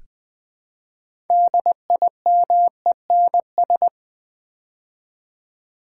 Morse code, heard as a keyed tone, about 20 wpm, 700 Hz. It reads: DIMENS
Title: Build A Word - 3 Letters At A Time - 9-Letter Words - 20wpm